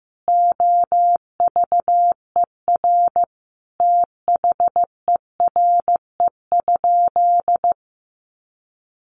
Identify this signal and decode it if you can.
Morse code, heard as a keyed tone, about 15 wpm, 700 Hz. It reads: OVER THERE?
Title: Sentences from Top 1500 Words 15wpm